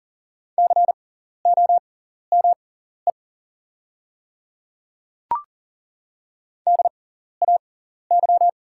Morse code, heard as a keyed tone, about 40 wpm, spaced as for 14 wpm, 700 Hz. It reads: COME DAY
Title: 6x Character Spacing - Top 300 Words - 40wpm